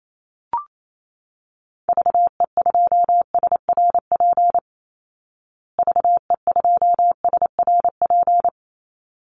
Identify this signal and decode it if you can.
Morse code, heard as a keyed tone, about 28 wpm, 700 Hz. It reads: VE2SRP VE2SRP
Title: Call Signs Repeated 3x - 28wpm